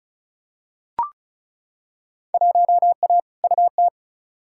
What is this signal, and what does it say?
Morse code, heard as a keyed tone, about 36 wpm, 700 Hz. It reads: 1A UT
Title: Field Day Exchanges - 35wpm